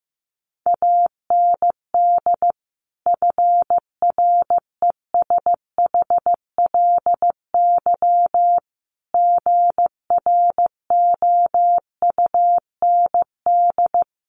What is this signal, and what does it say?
Morse code, heard as a keyed tone, about 15 wpm, 700 Hz. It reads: AND FRESHLY GROUND
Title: Sets of 5 Words Encore 15wpm